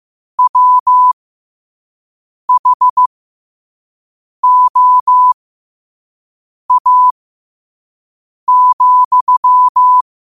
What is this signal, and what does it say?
Morse code, heard as a keyed tone, about 15 wpm, spaced as for 5 wpm, 1000 Hz. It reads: WHOA,
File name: qotdfiles / 5wpm